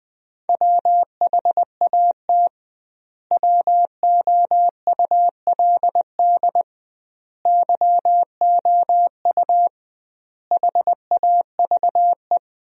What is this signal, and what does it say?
Morse code, heard as a keyed tone, about 20 wpm, 700 Hz. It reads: WHAT WOULD YOU HAVE
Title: 2x Word-Spacing - Sentences from Top 300 Words - 20wpm